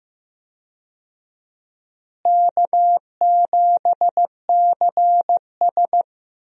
Morse code, heard as a keyed tone, about 16 wpm, 700 Hz. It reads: K7CS
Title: USA Call Signs - 4 Characters - Repeated 2x - 15wpm